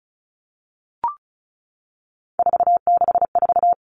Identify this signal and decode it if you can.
Morse code, heard as a keyed tone, about 34 wpm, 700 Hz. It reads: 464
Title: Numbers - 3-Digit 35wpm